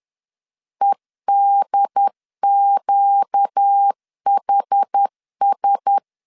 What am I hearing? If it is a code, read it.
Morse code, heard as a keyed tone, about 10 wpm, 800 Hz. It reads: EDQHS
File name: LEZIONI / 6 PKX0 / 6 P 10wpm 2432